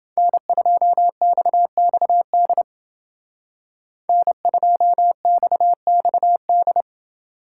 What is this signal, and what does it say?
Morse code, heard as a keyed tone, about 28 wpm, 700 Hz. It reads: N2XXD N2XXD